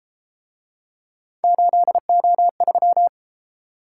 Morse code, heard as a keyed tone, about 33 wpm, 700 Hz. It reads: 8O3